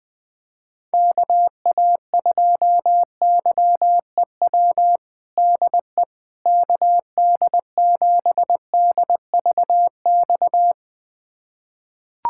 Morse code, heard as a keyed tone, about 21 wpm, 700 Hz. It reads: KA2YEW DE KD7DVX
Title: USA Callsign DE Callsign - 6 Characters - 20wpm